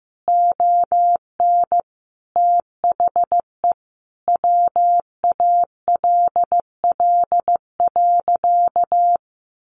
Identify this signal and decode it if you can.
Morse code, heard as a keyed tone, about 14 wpm, 700 Hz. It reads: ON THE WALL.
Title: Sentences from Top 1100 Words 15wpm